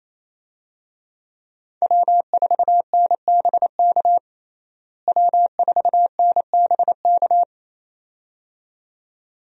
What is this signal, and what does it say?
Morse code, heard as a keyed tone, about 28 wpm, 700 Hz. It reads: W4NBK W4NBK
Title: USA Call Signs - 5 Characters - Repeated 2x - 28wpm